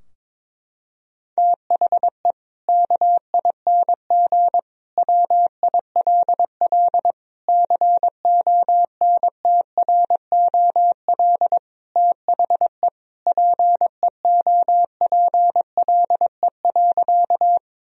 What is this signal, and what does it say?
Morse code, heard as a keyed tone, about 22 wpm, 700 Hz. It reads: THE KING WILL CONTROL THE PEOPLE.
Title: Sentences from Top 1000 Words - Review Part 3 - 22wpm